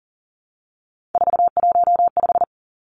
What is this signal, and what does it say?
Morse code, heard as a keyed tone, about 40 wpm, 700 Hz. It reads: 415
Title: Numbers - 3-Digit 40wpm